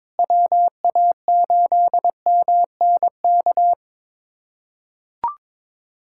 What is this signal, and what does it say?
Morse code, heard as a keyed tone, about 22 wpm, 700 Hz. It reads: WA8MNK
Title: USA Callsign DE Callsign 22wpm